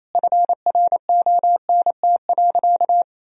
Morse code, heard as a keyed tone, about 28 wpm, 700 Hz. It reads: FRONT.
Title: Sentences from Top 1000 Words - Review Part 2 - 28wpm